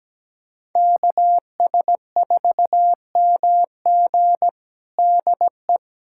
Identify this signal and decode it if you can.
Morse code, heard as a keyed tone, about 17 wpm, 700 Hz. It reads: KS4MG DE
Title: USA Callsign DE Callsign - 5 Characters - 17wpm